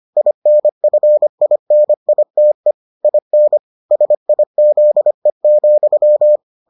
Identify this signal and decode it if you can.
Morse code, heard as a keyed tone, about 25 wpm, 600 Hz. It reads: INFINITE IN SIZE,